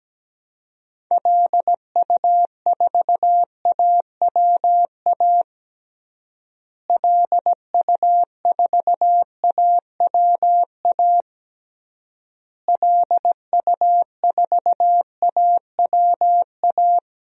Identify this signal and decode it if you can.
Morse code, heard as a keyed tone, about 17 wpm, 700 Hz. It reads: LU4AWA LU4AWA LU4AWA
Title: DXCC-100 Call Signs - 6 Characters - Repeated 3x - 17wpm